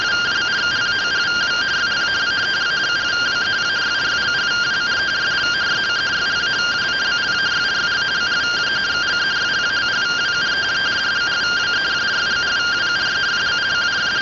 Two-tone teleprinter signal, about 170 Hz shift.